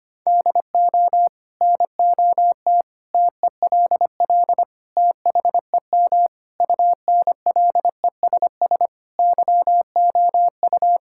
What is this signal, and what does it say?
Morse code, heard as a keyed tone, about 25 wpm, 700 Hz. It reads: DO NOT TELL THEM UNLESS YOU
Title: Sentences from Top 1000 Words - Review Part 6 - 25wpm